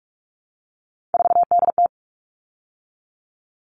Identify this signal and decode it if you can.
Morse code, heard as a keyed tone, about 44 wpm, 700 Hz. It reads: 4DT
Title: All Cut Numbers - 3-Digit 45wpm